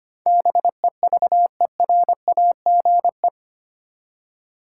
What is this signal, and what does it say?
Morse code, heard as a keyed tone, about 25 wpm, 700 Hz. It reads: BEVERAGE
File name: 8 Letter Words - First 3 Letters Given - 25wpm